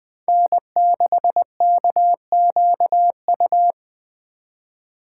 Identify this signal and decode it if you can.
Morse code, heard as a keyed tone, about 20 wpm, 700 Hz. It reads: N6KQU